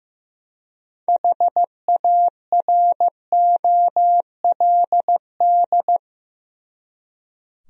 Morse code, heard as a keyed tone, about 15 wpm, 700 Hz. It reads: HAROLD
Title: USA Common Names - 15wpm